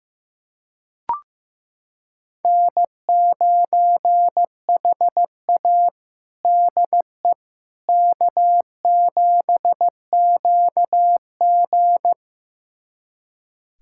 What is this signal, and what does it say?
Morse code, heard as a keyed tone, about 15 wpm, 700 Hz. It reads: N9HA DE K7QG
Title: USA Callsign DE Callsign - 4 Characters - 15wpm